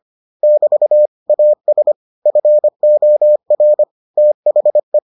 Morse code, heard as a keyed tone, about 25 wpm, 600 Hz. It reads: <BT> AS FOR THE